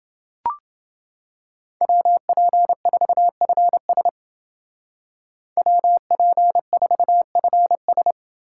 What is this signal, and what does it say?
Morse code, heard as a keyed tone, about 29 wpm, 700 Hz. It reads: WP4FS WP4FS